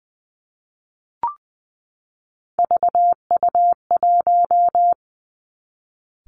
Morse code, heard as a keyed tone, about 19 wpm, 700 Hz. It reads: VU1
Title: Three Challenging Letter-Number Combinations 20wpm